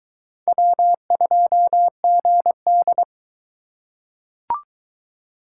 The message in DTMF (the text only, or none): none